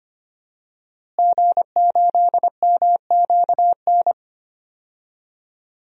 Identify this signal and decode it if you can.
Morse code, heard as a keyed tone, about 25 wpm, 700 Hz. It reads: G8MQN
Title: DXCC-100 Call Signs - 5 Characters - Repeated 3x - 25wpm